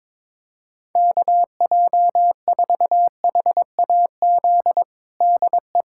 Morse code, heard as a keyed tone, about 22 wpm, 700 Hz. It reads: KJ4HAZ DE
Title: USA Callsign DE Callsign - 6 Characters - 22wpm